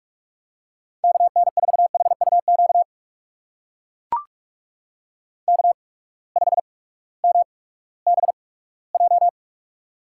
Morse code, heard as a keyed tone, about 44 wpm, spaced as for 20 wpm, 700 Hz. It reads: KN4HUQ K5MBJ
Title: USA Call Signs - 8x Character Spacing - 45wpm